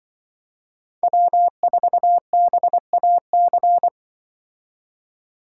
Morse code, heard as a keyed tone, about 24 wpm, 700 Hz. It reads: W4BAC
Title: Speed Racing - USA Call Signs 20wpm